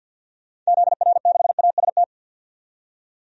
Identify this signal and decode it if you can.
Morse code, heard as a keyed tone, about 52 wpm, 700 Hz. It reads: ZR6AST
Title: DXCC-100 Call Signs - 6 Characters - 50wpm